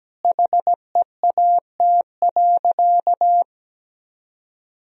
Morse code, heard as a keyed tone, about 17 wpm, 700 Hz. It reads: HEAT.
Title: Sentences from Top 1000 Words - Review Part 3 - 17wpm